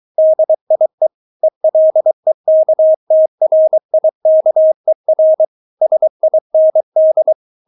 Morse code, heard as a keyed tone, about 23 wpm, 630 Hz. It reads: DIE ELEKTRIKER SIND
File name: Witz4 23wpm